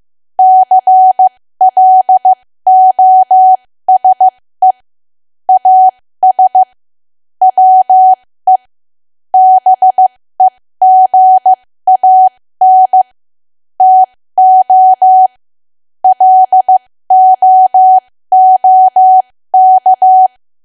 Morse code, heard as a keyed tone, about 15 wpm, spaced as for 13 wpm, 750 Hz. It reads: CLOSE AS WE BEGAN TO LOOK
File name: cwa / W1AW-Level2 / 13WPM